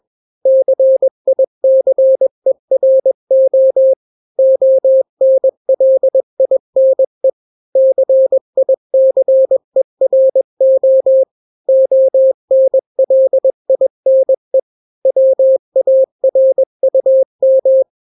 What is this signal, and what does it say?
Morse code, heard as a keyed tone, about 21 wpm, 530 Hz. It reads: CICERO ONLINE CICERO ONLINE WARUM
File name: daily / 20wpm